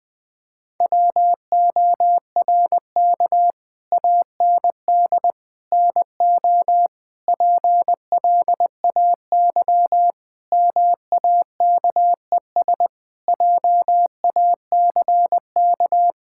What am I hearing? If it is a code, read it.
Morse code, heard as a keyed tone, about 21 wpm, 700 Hz. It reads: WORK AND NO PLAY MAKES JACK
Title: Sentences with 7-Word Phrases - 20wpm